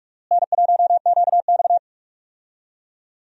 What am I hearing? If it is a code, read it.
Morse code, heard as a keyed tone, about 42 wpm, 700 Hz. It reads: N1QX
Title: USA Call Signs - 4 Characters - Repeated 3x - 45wpm